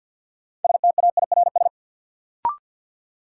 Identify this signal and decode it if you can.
Morse code, heard as a keyed tone, about 53 wpm, 700 Hz. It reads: STAIRS
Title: Sets of 5 Words Encore - New Word Drill - 50wpm